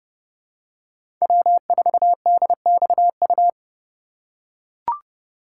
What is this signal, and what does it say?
Morse code, heard as a keyed tone, about 31 wpm, 700 Hz. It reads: W4DXU
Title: USA Call Signs - 5 Characters - 3x Character Spacing - 30wpm